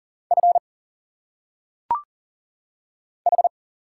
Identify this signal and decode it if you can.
Morse code, heard as a keyed tone, about 40 wpm, 700 Hz. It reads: F H
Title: Single Letters - Phonetic Pronunciation 40wpm